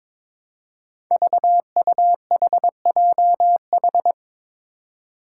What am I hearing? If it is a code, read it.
Morse code, heard as a keyed tone, about 21 wpm, 700 Hz. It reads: VUHJH
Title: Five Challenging Letter-Number Combinations 22wpm